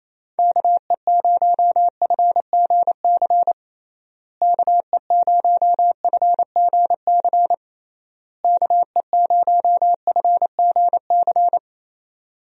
Morse code, heard as a keyed tone, about 28 wpm, 700 Hz. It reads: KE0FGC KE0FGC KE0FGC
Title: USA Call Signs - 6 Characters - Repeated 3x - 28wpm